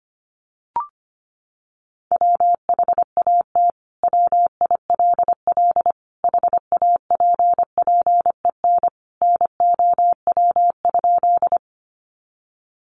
Morse code, heard as a keyed tone, about 25 wpm, 700 Hz. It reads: WHAT WILL HAPPEN NOW?
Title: Sentences from Top 400 Words 25wpm